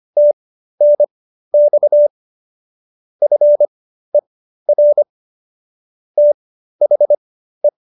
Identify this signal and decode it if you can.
Morse code, heard as a keyed tone, about 25 wpm, spaced as for 13 wpm, 590 Hz. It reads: TNX FER THE